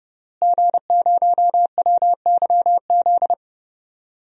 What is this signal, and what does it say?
Morse code, heard as a keyed tone, about 30 wpm, 700 Hz. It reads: G0WYZ
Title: DXCC-100 Call Signs - 5 Characters - Repeated 3x - 30wpm